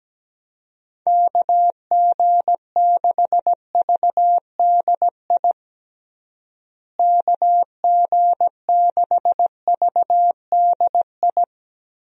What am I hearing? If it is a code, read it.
Morse code, heard as a keyed tone, about 17 wpm, 700 Hz. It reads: KG6VDI KG6VDI
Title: USA Call Signs - 6 Characters - Repeated 3x - 17wpm